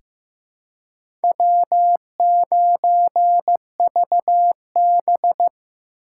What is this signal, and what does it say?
Morse code, heard as a keyed tone, about 15 wpm, 700 Hz. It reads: W9VB
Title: USA Call Signs - 4x Character Spacing - 15wpm